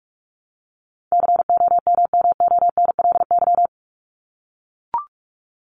Morse code, heard as a keyed tone, about 44 wpm, 700 Hz. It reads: COMMONLY
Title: LY Suffix - Top 100 Words with Base Words 45wpm